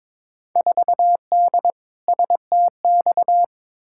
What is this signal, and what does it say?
Morse code, heard as a keyed tone, about 22 wpm, 700 Hz. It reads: 4D STX
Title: Field Day Exchanges Encore - 22wpm